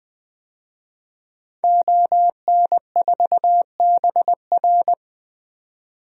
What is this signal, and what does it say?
Morse code, heard as a keyed tone, about 20 wpm, 700 Hz. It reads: ON4BR
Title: DXCC-100 Call Signs - 5 Characters - 20wpm